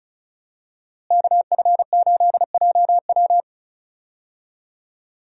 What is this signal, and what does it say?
Morse code, heard as a keyed tone, about 34 wpm, 700 Hz. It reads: KF8JW
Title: USA Call Signs - 5 Characters - Repeated 2x - 35wpm